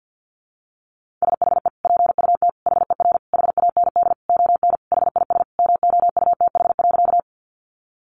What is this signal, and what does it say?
Morse code, heard as a keyed tone, about 50 wpm, 700 Hz. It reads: SHE PUT HER HAND ON HIS MOUTH.
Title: Sentences from Top 1500 Words 50wpm